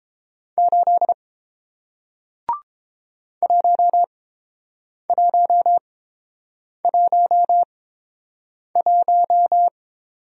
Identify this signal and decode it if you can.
Morse code, heard as a keyed tone, about 30 wpm, spaced as for 7 wpm, 700 Hz. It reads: 8 1111